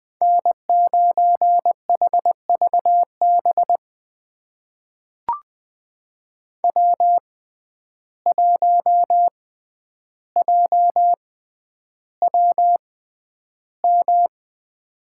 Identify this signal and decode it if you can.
Morse code, heard as a keyed tone, about 20 wpm, spaced as for 10 wpm, 700 Hz. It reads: N9HVB W1JWM
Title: USA Call Signs - 6x Character Spacing - 20wpm